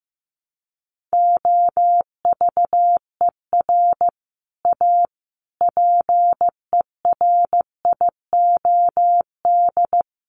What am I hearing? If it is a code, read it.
Morse code, heard as a keyed tone, about 15 wpm, 700 Hz. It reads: OVER A PERIOD
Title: Sets of 5 Words Encore2 15wpm